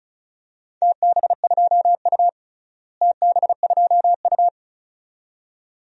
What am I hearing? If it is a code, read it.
Morse code, heard as a keyed tone, about 34 wpm, 690 Hz. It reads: TB2U TB2U
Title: DXCC-100 Call Signs - 4 Characters - Repeated 2x - 35wpm